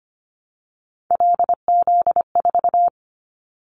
Morse code, heard as a keyed tone, about 25 wpm, 700 Hz. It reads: LZ4
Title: Callsign Prefixes for Top 100 DXCC - 25wpm